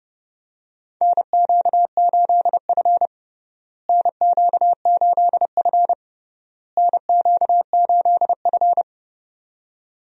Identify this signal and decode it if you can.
Morse code, heard as a keyed tone, about 32 wpm, 700 Hz. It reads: NQ8F NQ8F NQ8F
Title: USA Call Signs - 4 Characters - Repeated 3x - 30wpm